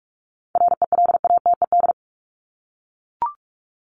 Morse code, heard as a keyed tone, about 44 wpm, 700 Hz. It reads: RELATED
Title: ED Suffix - Top 100 Words with Base Words 45wpm